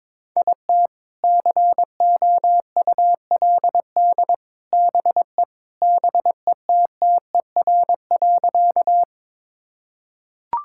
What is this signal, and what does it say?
Morse code, heard as a keyed tone, about 22 wpm, 700 Hz. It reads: IT COULD BE BETTER.
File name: practice / 22wpm / Sentences with 2-Word Phrases - 22wpm